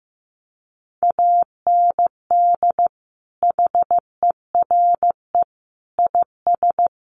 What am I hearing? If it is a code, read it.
Morse code, heard as a keyed tone, about 15 wpm, 700 Hz. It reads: AND HERE IS
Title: Sentences from Top 500 Words 15wpm